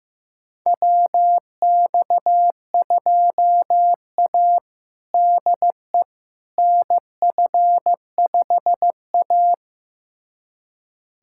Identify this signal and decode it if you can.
Morse code, heard as a keyed tone, about 15 wpm, 700 Hz. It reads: WX2A DE NF5A
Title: USA Callsign DE Callsign - 4 Characters - 15wpm